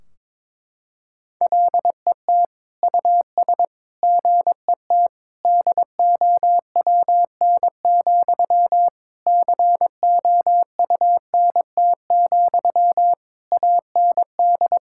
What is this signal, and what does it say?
Morse code, heard as a keyed tone, about 21 wpm, 700 Hz. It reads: LET US GET DOWN, COUNT, AND
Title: Sentences from Top 1600 Words 22wpm